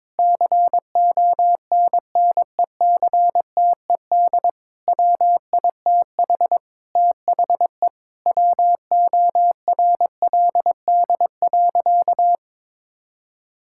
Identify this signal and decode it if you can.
Morse code, heard as a keyed tone, about 22 wpm, 700 Hz. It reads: CONNECTED WITH THE WORLD.
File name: Difficult Sentences with Hints - 22wpm